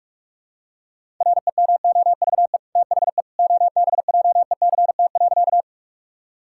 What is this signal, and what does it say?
Morse code, heard as a keyed tone, about 41 wpm, 700 Hz. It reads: REMOVE THE OBJECT.
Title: Sentences from Top 1000 Words - Review Part 5 - 45wpm